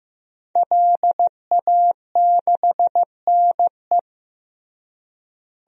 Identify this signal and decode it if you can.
Morse code, heard as a keyed tone, about 15 wpm, 700 Hz. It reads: LA6NE
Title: DXCC-100 Call Signs - 5 Characters - Repeated 3x - 15wpm